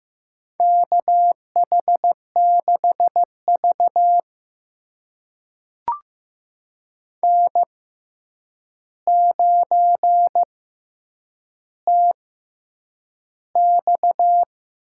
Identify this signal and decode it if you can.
Morse code, heard as a keyed tone, about 15 wpm, spaced as for 8 wpm, 700 Hz. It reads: KH6V N9TX
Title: USA Call Signs - 4 Characters - 6x Character Spacing - 15wpm